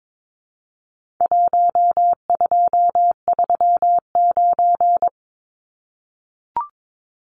Morse code, heard as a keyed tone, about 22 wpm, 700 Hz. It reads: 1239
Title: Numbers - 4-Digit 22wpm